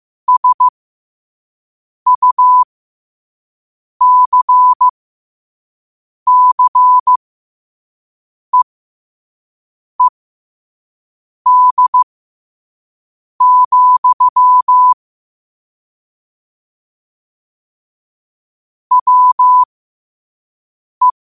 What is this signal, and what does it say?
Morse code, heard as a keyed tone, about 15 wpm, spaced as for 5 wpm, 1000 Hz. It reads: SUCCEED, WE